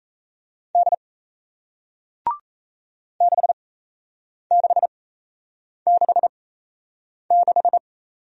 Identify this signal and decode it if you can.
Morse code, heard as a keyed tone, about 37 wpm, spaced as for 8 wpm, 700 Hz. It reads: D 6666